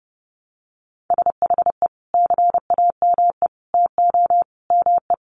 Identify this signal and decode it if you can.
Morse code, heard as a keyed tone, about 30 wpm, 700 Hz. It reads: SHE CAME TO ME
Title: Sentences from Top 1100 Words 30wpm